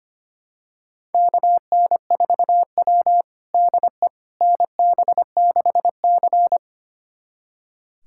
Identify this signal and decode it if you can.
Morse code, heard as a keyed tone, about 25 wpm, 700 Hz. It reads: KN4W DE NB6C
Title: USA Callsign DE Callsign - 4 Characters - 25wpm